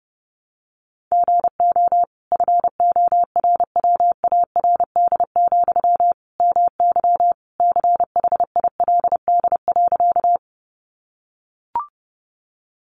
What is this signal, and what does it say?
Morse code, heard as a keyed tone, about 30 wpm, 700 Hz. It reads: GO FORWARD, MY CHILD.
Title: Sentences from Top 1300 Words 30wpm